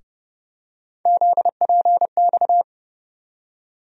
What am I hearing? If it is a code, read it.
Morse code, heard as a keyed tone, about 30 wpm, 700 Hz. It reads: ZPX
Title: USA Call Sign Postfixes - 3 Letters 30wpm